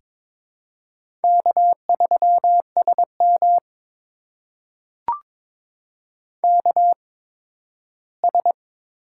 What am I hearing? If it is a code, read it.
Morse code, heard as a keyed tone, about 21 wpm, spaced as for 13 wpm, 700 Hz. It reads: K3SM KS
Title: USA Call Signs - 4 Characters - 8x Character Spacing - 22wpm